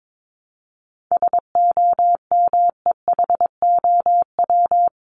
Morse code, heard as a keyed tone, about 22 wpm, 700 Hz. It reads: SOMEHOW
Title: Top 1501-1750 Words 22wpm